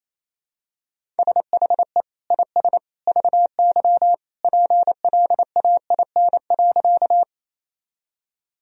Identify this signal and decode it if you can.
Morse code, heard as a keyed tone, about 28 wpm, 700 Hz. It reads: SHE IS VY PLAIN.
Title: Sentences from Top 400 Words and Abbreviations 28wpm